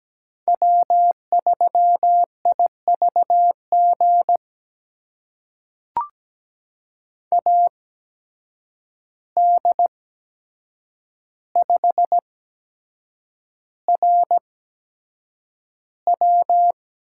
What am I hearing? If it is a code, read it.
Morse code, heard as a keyed tone, about 17 wpm, spaced as for 7 wpm, 700 Hz. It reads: W3IVG AD5RW
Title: USA Call Signs - 5 Characters - 8x Character Spacing - 17wpm